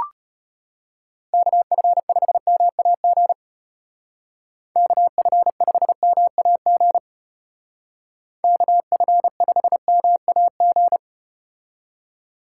Morse code, heard as a keyed tone, about 35 wpm, 700 Hz. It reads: KF5MAG KF5MAG KF5MAG